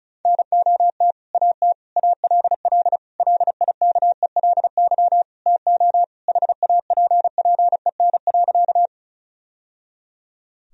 Morse code, heard as a keyed tone, about 35 wpm, 700 Hz. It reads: NOT AT ALL LIKELY TO HAPPEN.
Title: Sentences from Top 1000 Words - Review Part 4 - 35wpm